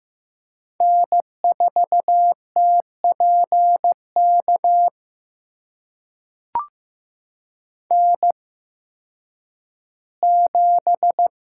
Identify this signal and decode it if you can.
Morse code, heard as a keyed tone, about 15 wpm, spaced as for 10 wpm, 700 Hz. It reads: N4TPK N7
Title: USA Call Signs - 5 Characters - 8x Character Spacing - 15wpm